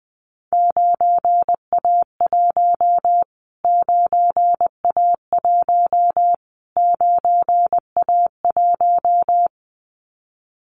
Morse code, heard as a keyed tone, about 20 wpm, 700 Hz. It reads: 9A1 9A1 9A1
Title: Callsign Prefixes for Top 25 DXCC - Repeated 3x - 20wpm